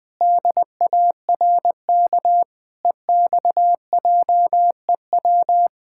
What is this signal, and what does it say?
Morse code, heard as a keyed tone, about 20 wpm, 700 Hz. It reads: DARK EXJEW
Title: Pangram Challenge v2 - 20wpm